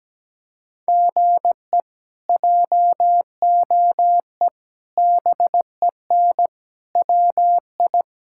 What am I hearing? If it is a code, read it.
Morse code, heard as a keyed tone, about 17 wpm, 700 Hz. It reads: GE JOE BEN WI
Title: K1USN SST QSO Practice - 17wpm